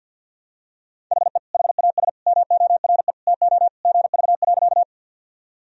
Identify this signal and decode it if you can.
Morse code, heard as a keyed tone, about 50 wpm, 700 Hz. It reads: HE HAS MORE TO GV.
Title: Sentences from Top 200 Words and Abbreviations 50wpm